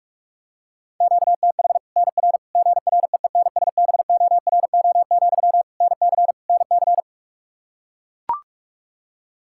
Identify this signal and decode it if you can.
Morse code, heard as a keyed tone, about 41 wpm, 700 Hz. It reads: QTH NR GREENSBORO, NC NC
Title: Practice QSO - Exchange 1 - Beginning - 45wpm